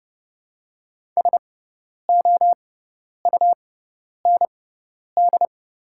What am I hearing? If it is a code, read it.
Morse code, heard as a keyed tone, about 30 wpm, spaced as for 10 wpm, 700 Hz. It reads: SOUND
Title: 6x Character Spacing - Top 200 Words - 30wpm